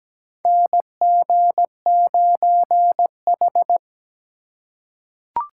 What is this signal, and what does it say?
Morse code, heard as a keyed tone, about 17 wpm, 700 Hz. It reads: NG9H
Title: USA Call Signs - 3x Character Spacing - 17wpm